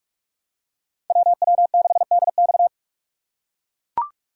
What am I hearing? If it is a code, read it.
Morse code, heard as a keyed tone, about 43 wpm, 700 Hz. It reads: WW6DX